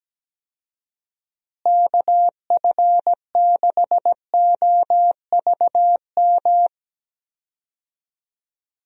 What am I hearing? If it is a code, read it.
Morse code, heard as a keyed tone, about 17 wpm, 700 Hz. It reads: KF6OVM
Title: USA Call Signs - 6 Characters - Repeated 2x - 17wpm